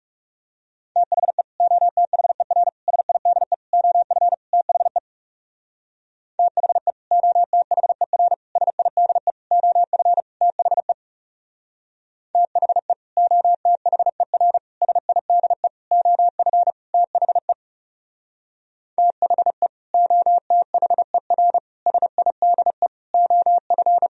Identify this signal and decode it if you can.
Morse code, heard as a keyed tone, about 37 wpm, 700 Hz. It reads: THE OTHER SIDE OF THE THE OTHER SIDE OF THE THE OTHER SIDE OF THE THE OTHER SIDE OF
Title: Speed-Racing - Sets of 5 Words 30wpm